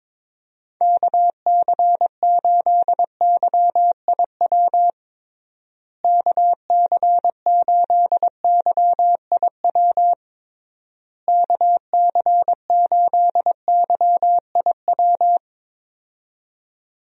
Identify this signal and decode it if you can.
Morse code, heard as a keyed tone, about 22 wpm, 700 Hz. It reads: KC8YIW KC8YIW KC8YIW
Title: USA Call Signs - 6 Characters - Repeated 3x - 22wpm